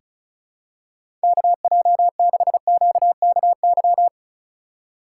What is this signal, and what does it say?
Morse code, heard as a keyed tone, about 36 wpm, 700 Hz. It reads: KJ6QKY
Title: USA Call Signs - 6 Characters - 2x Character Spacing - 35wpm